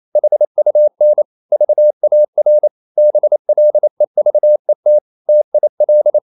Morse code, heard as a keyed tone, about 28 wpm, 600 Hz. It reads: HUN VAR BLEVET TIL
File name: Ferie QRQ 28wpm